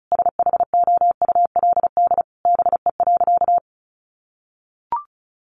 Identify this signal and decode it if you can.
Morse code, heard as a keyed tone, about 34 wpm, 700 Hz. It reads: SHOULD BE.
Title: Sentences from Top 900 Words 35wpm